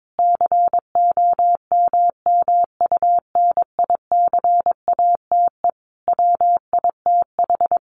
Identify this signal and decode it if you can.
Morse code, heard as a keyed tone, about 22 wpm, 700 Hz. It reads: COMMUNICATE WITH
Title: Sentences from Top 1300 Words 22wpm